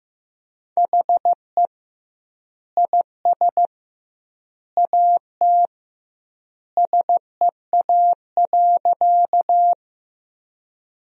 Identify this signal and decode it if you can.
Morse code, heard as a keyed tone, about 15 wpm, 700 Hz. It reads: HE IS AT SEA.
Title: 2x Word-Spacing - Sentences from Top 300 Words - 15wpm